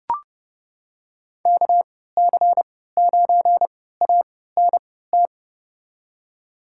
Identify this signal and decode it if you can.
Morse code, heard as a keyed tone, about 30 wpm, spaced as for 17 wpm, 700 Hz. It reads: KC9ANT
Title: USA Call Signs - 6 Characters - 3x Character Spacing - 30wpm